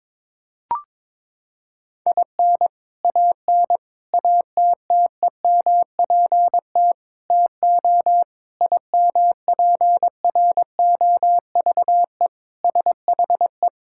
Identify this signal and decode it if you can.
Morse code, heard as a keyed tone, about 22 wpm, 700 Hz. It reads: IN AN ATTEMPT TO IMPROVE SHE
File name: practice / 22wpm / Sentences with 4-Word Phrases - 22wpm